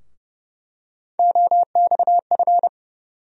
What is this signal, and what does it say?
Morse code, heard as a keyed tone, about 30 wpm, 700 Hz. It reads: OXF
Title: USA Call Sign Postfixes - 3 Letters 30wpm